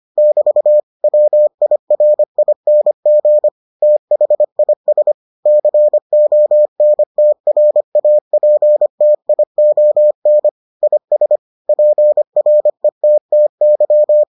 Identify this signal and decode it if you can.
Morse code, heard as a keyed tone, about 25 wpm, 600 Hz. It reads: <BT> WIRING THIS CONTRAPTION IS PRETTY